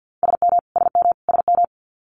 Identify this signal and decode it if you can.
Morse code, heard as a keyed tone, about 50 wpm, 700 Hz. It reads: SM SM SM
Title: Multi-letter Phonemes Repeated 3x 50wpm